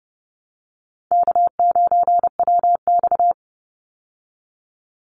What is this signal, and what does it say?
Morse code, heard as a keyed tone, about 30 wpm, 700 Hz. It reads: K9WX
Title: Easy Call Signs Repeated 3x - 30wpm